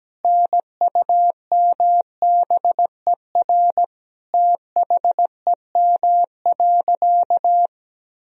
Morse code, heard as a keyed tone, about 17 wpm, 700 Hz. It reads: NUMBER THEM.
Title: Sentences from Top 1000 Words - Review Part 1 - 17wpm